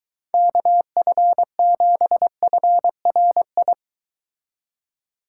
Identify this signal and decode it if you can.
Morse code, heard as a keyed tone, about 22 wpm, 700 Hz. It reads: KF7FRI